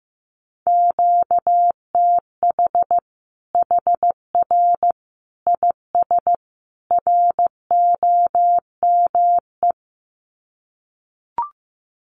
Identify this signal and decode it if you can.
Morse code, heard as a keyed tone, about 15 wpm, 700 Hz. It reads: QTH HR IS ROME
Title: Common QSO Phrases 15wpm